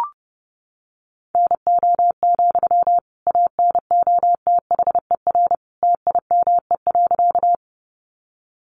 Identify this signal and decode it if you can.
Morse code, heard as a keyed tone, about 30 wpm, 700 Hz. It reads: NO, ANOTHER TIME.
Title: Sentences from Top 600 Words 30wpm